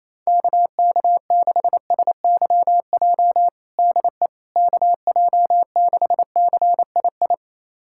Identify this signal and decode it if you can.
Morse code, heard as a keyed tone, about 28 wpm, 700 Hz. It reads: KK6SYJ DE KJ6CII
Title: USA Callsign DE Callsign - 6 Characters - 28wpm